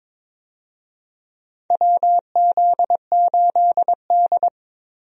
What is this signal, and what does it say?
Morse code, heard as a keyed tone, about 22 wpm, 700 Hz. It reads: WZ8D
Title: USA Call Signs - 4 Characters - 22wpm